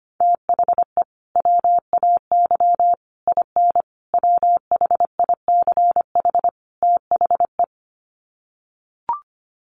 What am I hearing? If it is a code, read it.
Morse code, heard as a keyed tone, about 25 wpm, 700 Hz. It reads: THE WAY IN WHICH THE
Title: Sets of 5 Words Encore2 25wpm